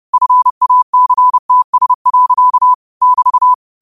Morse code, heard as a keyed tone, about 30 wpm, 1000 Hz. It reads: FACTS. <BT>